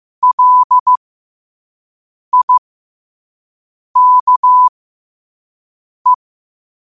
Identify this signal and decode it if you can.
Morse code, heard as a keyed tone, about 15 wpm, spaced as for 5 wpm, 1000 Hz. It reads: LIKE
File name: 5wpm